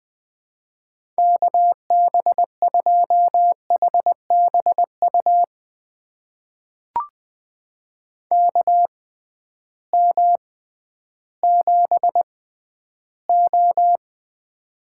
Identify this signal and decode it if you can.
Morse code, heard as a keyed tone, about 20 wpm, spaced as for 12 wpm, 700 Hz. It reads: KB2HBU KM7O
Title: USA Call Signs - 6x Character Spacing - 20wpm